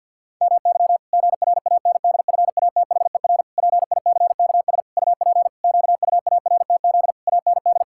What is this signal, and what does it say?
Morse code, heard as a keyed tone, about 51 wpm, 700 Hz. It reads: MY GRANDFATHER PICKS UP QUARTZ AND
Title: Pangram Challenge v2 - 50wpm